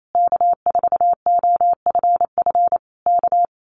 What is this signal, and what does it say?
Morse code, heard as a keyed tone, about 28 wpm, 700 Hz. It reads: K4OFF K
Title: Common QSO Phrases Encore 28wpm